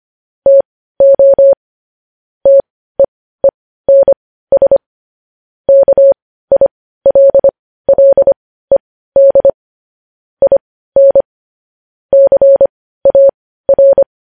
Morse code, heard as a keyed tone, about 25 wpm, spaced as for 15 wpm, 550 Hz. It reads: TO TEENS KILLED IN CAR
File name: podcast / News headlines-1x-25x15